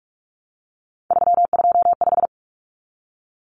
Morse code, heard as a keyed tone, about 44 wpm, 700 Hz. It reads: 325
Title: Numbers - 3-Digit 45wpm